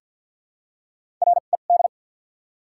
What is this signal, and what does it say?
Morse code, heard as a keyed tone, about 48 wpm, spaced as for 35 wpm, 700 Hz. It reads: RED
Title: 2x Character Spacing - Top 500 Words - 50wpm